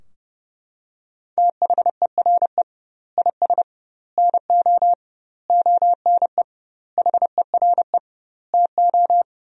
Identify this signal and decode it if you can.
Morse code, heard as a keyed tone, about 32 wpm, 700 Hz. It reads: THERE IS NO ONE HERE TO
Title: 2x Word-Spacing - Sentences from Top 200 Words - 30wpm